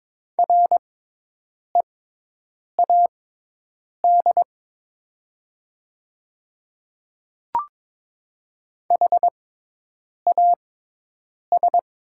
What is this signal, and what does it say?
Morse code, heard as a keyed tone, about 22 wpm, spaced as for 8 wpm, 700 Hz. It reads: READ HAS
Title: 6x Character Spacing - Top 200 Words - 22wpm